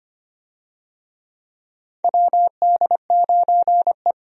Morse code, heard as a keyed tone, about 25 wpm, 700 Hz. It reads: WD9E
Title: USA Call Signs - 4 Characters - 25wpm